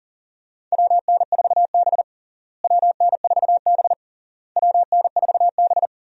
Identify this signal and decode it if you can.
Morse code, heard as a keyed tone, about 40 wpm, 700 Hz. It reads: WN4B WN4B WN4B
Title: USA Call Signs - 4 Characters - Repeated 3x - 40wpm